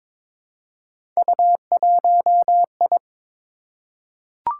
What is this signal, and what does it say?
Morse code, heard as a keyed tone, about 22 wpm, 700 Hz. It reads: U1I
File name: practice / 22wpm / Three Letter-Number Combinations - 22wpm